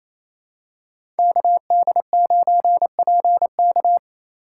Morse code, heard as a keyed tone, about 28 wpm, 700 Hz. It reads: KD9PK
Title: USA Call Signs - 5 Characters - 4x Character Spacing - 28wpm